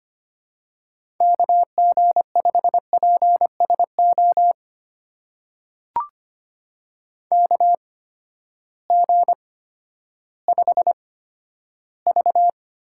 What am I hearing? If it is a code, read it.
Morse code, heard as a keyed tone, about 25 wpm, spaced as for 12 wpm, 700 Hz. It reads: KG5PSO KG5V
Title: USA Call Signs - 6 Characters - 8x Character Spacing - 25wpm